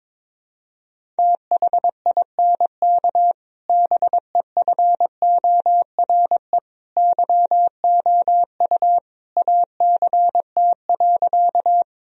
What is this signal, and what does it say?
Morse code, heard as a keyed tone, about 22 wpm, 700 Hz. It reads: THINK BEFORE YOU ACT.
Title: Sentences from Top 1000 Words - Review Part 1 - 22wpm